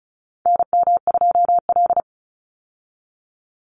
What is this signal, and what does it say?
Morse code, heard as a keyed tone, about 34 wpm, 700 Hz. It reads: NM2L
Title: Call Signs Repeated 3x - 35wpm